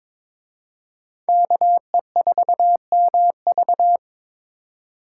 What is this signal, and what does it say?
Morse code, heard as a keyed tone, about 26 wpm, 700 Hz. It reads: KE4MV
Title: Speed Racing - USA 2x2 Call Signs 22wpm